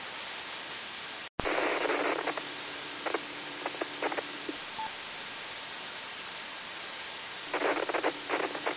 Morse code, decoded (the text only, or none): none